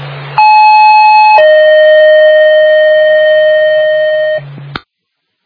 Two-tone page, heard as a one-second tone, then a three-second tone, about 860 then 620 Hz, with noise. It is heard from 0.4 to 4.4 s.